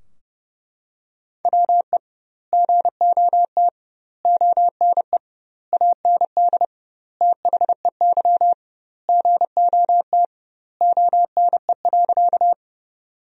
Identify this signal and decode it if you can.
Morse code, heard as a keyed tone, about 30 wpm, 700 Hz. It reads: WE GOT ONE AND THEY GOT ONE.
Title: 2x Word-Spacing - Sentences from Top 300 Words - 30wpm